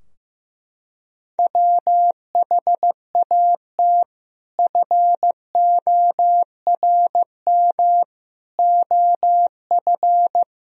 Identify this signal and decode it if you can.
Morse code, heard as a keyed tone, about 15 wpm, 700 Hz. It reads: WHAT FORM OF
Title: Sentences from Top 1000 Words - Review Part 3 - 15wpm